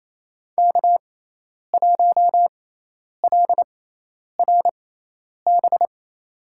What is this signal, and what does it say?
Morse code, heard as a keyed tone, about 28 wpm, spaced as for 10 wpm, 700 Hz. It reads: K1LRB
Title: USA Call Signs - 6x Character Spacing - 28wpm